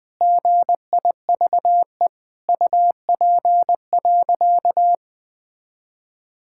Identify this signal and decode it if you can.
Morse code, heard as a keyed tone, about 21 wpm, 700 Hz. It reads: GIVE UP.
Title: Sentences with 5-Word Phrases - Spoken Beginning - 20wpm